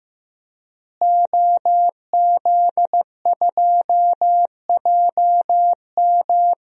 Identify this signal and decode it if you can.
Morse code, heard as a keyed tone, about 16 wpm, 700 Hz. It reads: OZ2JM
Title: DXCC-100 Call Signs - 5 Characters - Repeated 3x - 15wpm